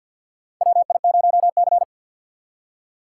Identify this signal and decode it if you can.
Morse code, heard as a keyed tone, about 51 wpm, 700 Hz. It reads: WI0C
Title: USA Call Signs - 4 Characters - 3x Character Spacing - 50wpm